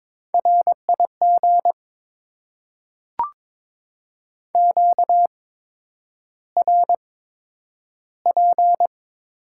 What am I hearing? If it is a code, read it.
Morse code, heard as a keyed tone, about 22 wpm, spaced as for 9 wpm, 700 Hz. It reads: RIG QRP
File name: Top 90 QSO Elements - 8x Character Spacing - 22wpm